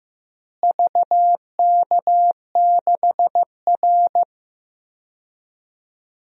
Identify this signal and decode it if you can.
Morse code, heard as a keyed tone, about 15 wpm, 700 Hz. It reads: VK6R
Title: DXCC-100 Call Signs - 4 Characters - Repeated 2x - 15wpm